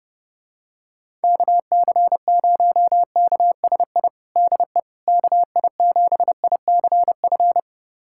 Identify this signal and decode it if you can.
Morse code, heard as a keyed tone, about 31 wpm, 700 Hz. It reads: KC0KSI DE KI7ICF
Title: USA Callsign DE Callsign - 6 Characters - 30wpm